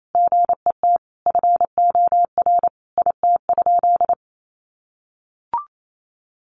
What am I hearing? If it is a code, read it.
Morse code, heard as a keyed tone, about 28 wpm, 700 Hz. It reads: GET FOR IT?
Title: Sentences from Top 400 Words 28wpm